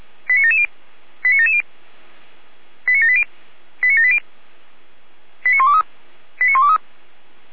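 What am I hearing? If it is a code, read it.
Five-tone selective call (ZVEI), heard as quick runs of five tones, tones of about 70 ms. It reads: 89770 89770 89780 89780 89123 89123
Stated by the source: ZVEI five-tone selective call, code 89760, received by SDR.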